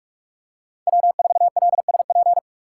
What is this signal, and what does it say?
Morse code, heard as a keyed tone, about 44 wpm, 700 Hz. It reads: W4LSP
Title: USA Call Signs - 5 Characters - 6x Character Spacing - 45wpm